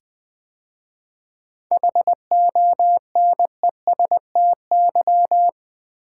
Morse code, heard as a keyed tone, about 20 wpm, 700 Hz. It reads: HONESTY
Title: 7 Letter Words - First 3 Letters Given - 20wpm